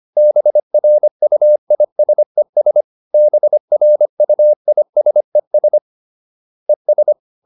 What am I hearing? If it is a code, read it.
Morse code, heard as a keyed tone, about 25 wpm, 600 Hz. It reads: BRUISES BRUISES ES